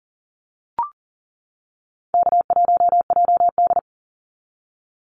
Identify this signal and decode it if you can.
Morse code, heard as a keyed tone, about 40 wpm, 700 Hz. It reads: K1JD
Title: NA SOTA Activator Callsigns 40wpm